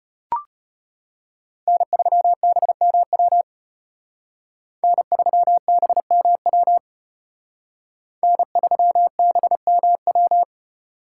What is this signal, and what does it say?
Morse code, heard as a keyed tone, about 31 wpm, 700 Hz. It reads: N3BMW N3BMW N3BMW
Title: Speed Racing - USA 1x3 Call Signs 25wpm